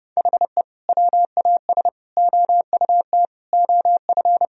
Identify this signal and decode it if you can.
Morse code, heard as a keyed tone, about 30 wpm, 700 Hz. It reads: HE WAS OUT OF
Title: Sentences from Top 1000 Words - Review Part 4 - 30wpm